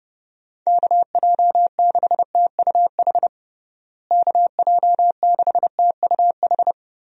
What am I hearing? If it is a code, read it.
Morse code, heard as a keyed tone, about 32 wpm, 700 Hz. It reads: KJ6TUH KJ6TUH
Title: USA Call Signs - 6 Characters - Repeated 3x - 30wpm